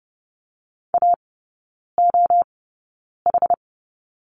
Morse code, heard as a keyed tone, about 30 wpm, spaced as for 9 wpm, 700 Hz. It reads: AOH
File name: Lesson 19e - Letter w - Three Characters 30wpm